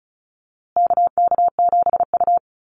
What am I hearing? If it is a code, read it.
Morse code, heard as a keyed tone, about 34 wpm, 700 Hz. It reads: KK7U
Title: USA Call Signs Repeated 2x 35wpm